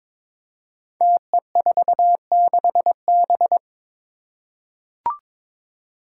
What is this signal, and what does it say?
Morse code, heard as a keyed tone, about 21 wpm, 700 Hz. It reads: TE46B
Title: Five Challenging Letter-Number Combinations 22wpm